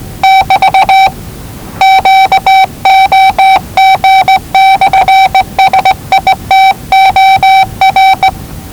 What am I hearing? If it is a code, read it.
Morse code, heard as a keyed tone, about 20 wpm, 770 Hz. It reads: <BT> QOG/SITOR
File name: WellingtonRadioZlwCloseDownOnHfAndMf / 01-Zlw 4277khz 417.5 qrt until 0925z